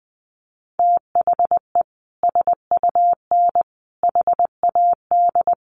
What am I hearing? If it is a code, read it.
Morse code, heard as a keyed tone, about 20 wpm, 700 Hz. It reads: THE SUN HAD
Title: Sentences from Top 300 Words 20wpm